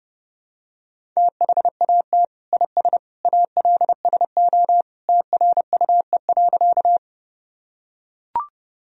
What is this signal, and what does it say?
Morse code, heard as a keyed tone, about 30 wpm, 700 Hz. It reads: THAT IS ALSO TRUE.
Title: Sentences from Top 1000 Words - Review Part 1 - 30wpm